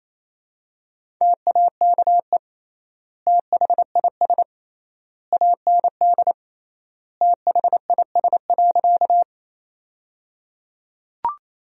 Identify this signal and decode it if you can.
Morse code, heard as a keyed tone, about 28 wpm, 700 Hz. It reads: TAKE THIS AND THIS.
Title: 3x Word-Spacing - Sentences from Top 200 Words - 28wpm